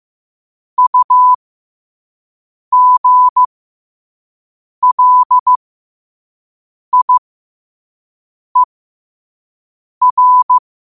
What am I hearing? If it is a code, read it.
Morse code, heard as a keyed tone, about 15 wpm, spaced as for 5 wpm, 1000 Hz. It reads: UGLIER
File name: qotdfiles / 5wpm